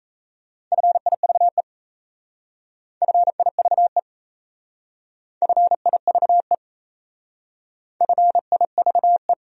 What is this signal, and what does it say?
Morse code, heard as a keyed tone, about 36 wpm, 700 Hz. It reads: FIVE FIVE FIVE FIVE